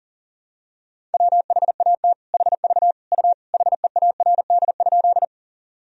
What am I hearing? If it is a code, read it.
Morse code, heard as a keyed tone, about 37 wpm, 700 Hz. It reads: WHAT HV U HEARD?
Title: Sentences from Top 400 Words and Abbreviations 40wpm